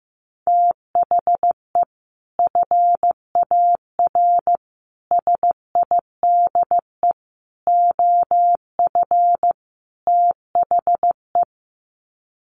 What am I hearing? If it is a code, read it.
Morse code, heard as a keyed tone, about 15 wpm, 700 Hz. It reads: THE FAR SIDE OF THE
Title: Sets of 5 Words 15wpm